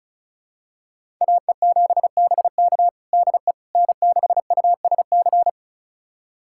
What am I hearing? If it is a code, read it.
Morse code, heard as a keyed tone, about 34 wpm, 700 Hz. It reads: AE7BK DE N6USC
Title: USA Callsign DE Callsign - 5 Characters - 35wpm